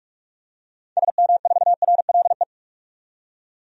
Morse code, heard as a keyed tone, about 43 wpm, 690 Hz. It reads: SM4RLE
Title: DXCC-100 Call Signs - 6 Characters - 45wpm